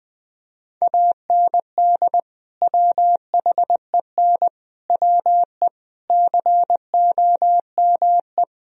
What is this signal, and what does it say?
Morse code, heard as a keyed tone, about 21 wpm, 700 Hz. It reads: AND WHEN WE COME
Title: Speed-Racing - Sets of 5 Words 20wpm